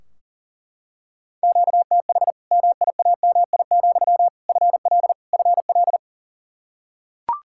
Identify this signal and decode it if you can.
Morse code, heard as a keyed tone, about 40 wpm, 700 Hz. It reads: QTH MIAMI, FL FL
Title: Practice QSO - Exchange 1 - Beginning - 40wpm